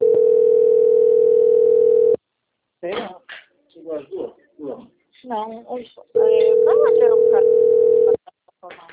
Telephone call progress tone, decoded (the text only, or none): ringback tone